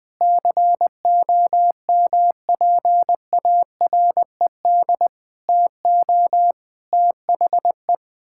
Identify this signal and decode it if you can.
Morse code, heard as a keyed tone, about 20 wpm, 700 Hz. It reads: COMPARED TO THE
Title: Sets of 8 Words - 20wpm